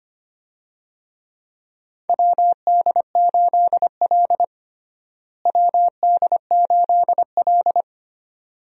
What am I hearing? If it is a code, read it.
Morse code, heard as a keyed tone, about 25 wpm, 700 Hz. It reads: WD8L WD8L
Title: USA Call Signs - 4 Characters - Repeated 2x - 25wpm